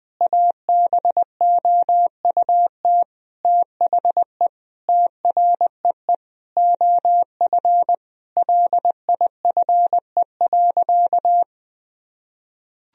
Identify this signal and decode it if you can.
Morse code, heard as a keyed tone, about 20 wpm, 700 Hz. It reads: ABOUT THE TREE OF LIFE.
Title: Sentences from Top 1000 Words - Review Part 1 - 20wpm